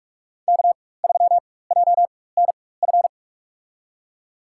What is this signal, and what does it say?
Morse code, heard as a keyed tone, about 45 wpm, spaced as for 21 wpm, 700 Hz. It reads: K3JNF
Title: USA Call Signs - 5 Characters - 4x Character Spacing - 45wpm